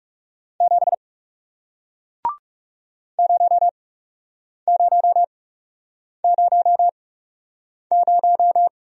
Morse code, heard as a keyed tone, about 41 wpm, spaced as for 8 wpm, 700 Hz. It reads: 7 0000